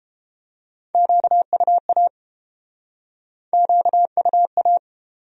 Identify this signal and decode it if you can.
Morse code, heard as a keyed tone, about 31 wpm, 700 Hz. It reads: QUA QUA